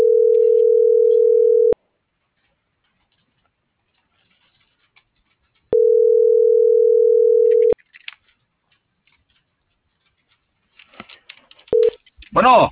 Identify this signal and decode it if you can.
Ringback tone (North America), 440 + 480 Hz.